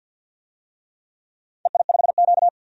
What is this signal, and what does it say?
Morse code, heard as a keyed tone, about 51 wpm, 700 Hz. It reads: EI5Q
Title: DXCC-100 Call Signs - 4 Characters - 50wpm